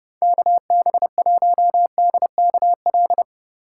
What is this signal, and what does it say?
Morse code, heard as a keyed tone, about 34 wpm, 700 Hz. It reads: KB1DKL